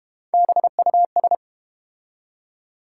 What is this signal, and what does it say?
Morse code, heard as a keyed tone, about 32 wpm, 700 Hz. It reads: BUS